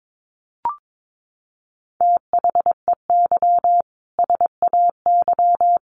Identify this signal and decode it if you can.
Morse code, heard as a keyed tone, about 22 wpm, 700 Hz. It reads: THEY SAY
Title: Additional Easy Phrases 22wpm